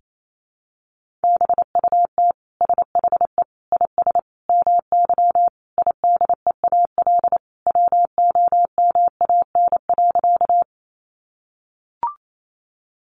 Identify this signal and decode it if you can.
Morse code, heard as a keyed tone, about 28 wpm, 700 Hz. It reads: BUT SHE IS MY IDEAL WOMAN.
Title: Sentences from Top 1400 Words 28wpm